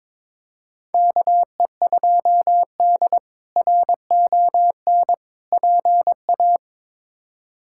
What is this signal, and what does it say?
Morse code, heard as a keyed tone, about 23 wpm, 700 Hz. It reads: KE2D RON PA
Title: K1USN SST QSO Practice - 22wpm